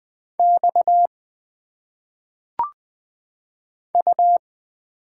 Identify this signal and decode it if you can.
Morse code, heard as a keyed tone, about 20 wpm, 700 Hz. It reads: X U
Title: Single Letters - Phonetic Pronunciation 20wpm